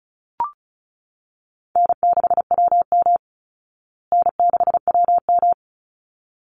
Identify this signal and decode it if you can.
Morse code, heard as a keyed tone, about 34 wpm, 700 Hz. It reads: N6WM N6WM
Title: Call Signs Repeated 3x - 35wpm